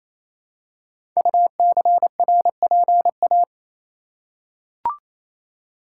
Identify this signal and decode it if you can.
Morse code, heard as a keyed tone, about 28 wpm, 700 Hz. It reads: UCRPA
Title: Code Groups - 5 Letter:Numbers - 28wpm